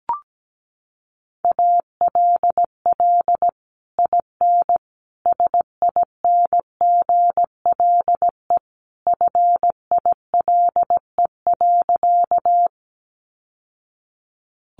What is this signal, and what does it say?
Morse code, heard as a keyed tone, about 17 wpm, 700 Hz. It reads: ALL IN SINGLE FILE.
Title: Sentences from Top 1200 Words 17wpm